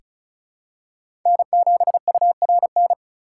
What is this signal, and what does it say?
Morse code, heard as a keyed tone, about 36 wpm, 700 Hz. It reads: N7URN
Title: USA Call Signs - 5 Characters - 4x Character Spacing - 35wpm